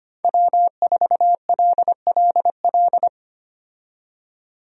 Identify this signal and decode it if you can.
Morse code, heard as a keyed tone, about 25 wpm, 700 Hz. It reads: W4LLL
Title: USA Call Signs - 5 Characters - 25wpm